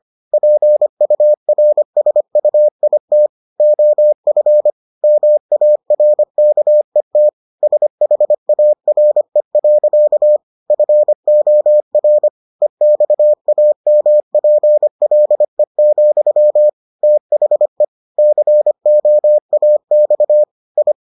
Morse code, heard as a keyed tone, about 25 wpm, 600 Hz. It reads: PURSUIT OF MARKET SHARE. FOR EXAMPLE, THE COAX I